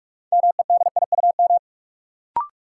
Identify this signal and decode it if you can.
Morse code, heard as a keyed tone, about 44 wpm, 700 Hz. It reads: MEDIUM
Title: Sentences from Top 900 Words - New Word Drill - 45wpm